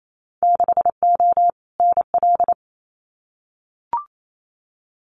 Morse code, heard as a keyed tone, about 28 wpm, 700 Hz. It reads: 6O NL
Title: Winter Field Day Exchanges Encore 28wpm